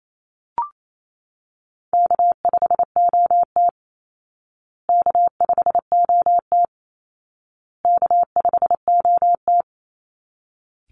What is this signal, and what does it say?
Morse code, heard as a keyed tone, about 28 wpm, 700 Hz. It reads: K5OT K5OT K5OT
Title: Easy Call Signs Repeated 3x - 28wpm